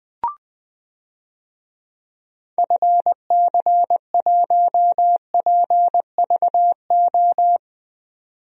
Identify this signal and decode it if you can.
Morse code, heard as a keyed tone, about 21 wpm, 700 Hz. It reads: FC1PVO
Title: DXCC-100 Call Signs - 6 Characters - 20wpm